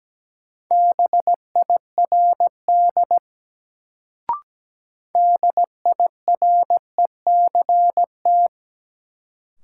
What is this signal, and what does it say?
Morse code, heard as a keyed tone, about 17 wpm, 700 Hz. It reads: BIRD DIRECT
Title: Sentences from Top 400 Words - New Word Drill - 17wpm